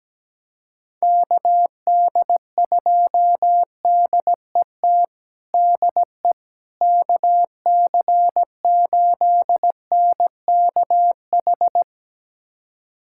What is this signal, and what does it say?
Morse code, heard as a keyed tone, about 17 wpm, 700 Hz. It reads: KD2DET DE KC8NKH
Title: USA Callsign DE Callsign - 6 Characters - 17wpm